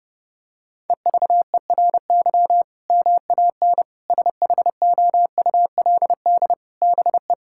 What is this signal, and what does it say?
Morse code, heard as a keyed tone, about 30 wpm, 700 Hz. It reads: EVERY MAN SHOULD BE
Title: Sentences from Top 1000 Words - Review Part 6 - 30wpm